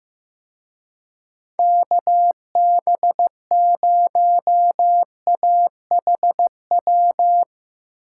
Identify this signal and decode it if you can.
Morse code, heard as a keyed tone, about 17 wpm, 700 Hz. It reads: KB0AHW
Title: Speed Racing - USA Call Signs 15wpm